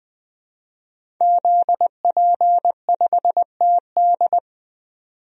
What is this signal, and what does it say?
Morse code, heard as a keyed tone, about 21 wpm, 700 Hz. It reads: ZP5TD
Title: DXCC-100 Call Signs - 5 Characters - 20wpm